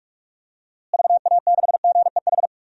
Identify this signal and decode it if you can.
Morse code, heard as a keyed tone, about 46 wpm, 700 Hz. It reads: VA6GEH
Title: DXCC-100 Call Signs - 6 Characters - Repeated 3x - 45wpm